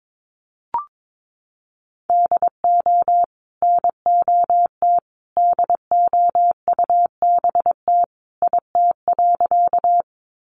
Morse code, heard as a keyed tone, about 22 wpm, 700 Hz. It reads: DO NOT DOUBT IT.
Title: Sentences from Top 1500 Words 22wpm